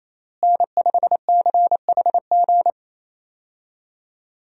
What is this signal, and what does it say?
Morse code, heard as a keyed tone, about 28 wpm, 700 Hz. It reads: N5CHG
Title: USA Call Signs - 5 Characters - Repeated 2x - 28wpm